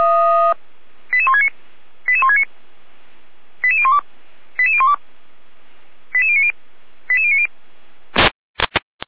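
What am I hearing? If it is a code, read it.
Five-tone selective call (ZVEI), heard as quick runs of five tones, tones of about 70 ms. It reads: 88169 88169 88012 88012 88090 88090
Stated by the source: ZVEI five-tone selective call received on an SDR.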